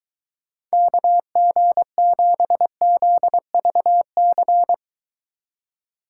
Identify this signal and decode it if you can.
Morse code, heard as a keyed tone, about 23 wpm, 700 Hz. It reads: KG7ZVC